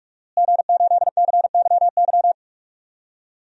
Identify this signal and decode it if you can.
Morse code, heard as a keyed tone, about 44 wpm, 700 Hz. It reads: G8CYY
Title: DXCC-100 Call Signs - 5 Characters - Repeated 3x - 45wpm